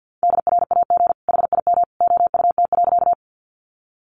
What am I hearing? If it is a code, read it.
Morse code, heard as a keyed tone, about 50 wpm, 700 Hz. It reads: DRAG HIM OUT.
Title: Sentences from Top 1500 Words 50wpm